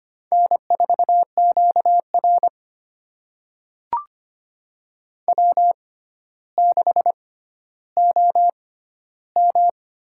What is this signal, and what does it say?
Morse code, heard as a keyed tone, about 25 wpm, spaced as for 13 wpm, 700 Hz. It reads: N4QR W6OM
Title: USA Call Signs - 4 Characters - 6x Character Spacing - 25wpm